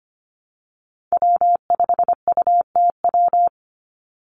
Morse code, heard as a keyed tone, about 25 wpm, 700 Hz. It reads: W5UTW
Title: Call Signs - 25wpm